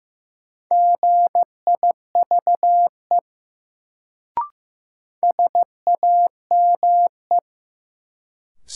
Morse code, heard as a keyed tone, about 15 wpm, 700 Hz. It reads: GIVE SAME
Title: Sentences from Top 200 Words - New Word Drill - 15wpm